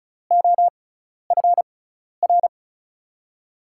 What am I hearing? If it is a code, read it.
Morse code, heard as a keyed tone, about 36 wpm, spaced as for 12 wpm, 700 Hz. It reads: OFR
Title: USA Call Signs - 6x Character Spacing - 35wpm